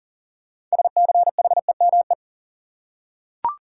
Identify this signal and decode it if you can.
Morse code, heard as a keyed tone, about 40 wpm, 700 Hz. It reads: SCHEME
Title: Six Letter Words - 40wpm